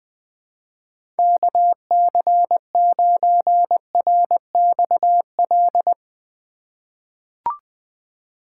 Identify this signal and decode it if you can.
Morse code, heard as a keyed tone, about 21 wpm, 700 Hz. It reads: KC9RXL
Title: USA Call Signs - 6 Characters - 3x Character Spacing - 20wpm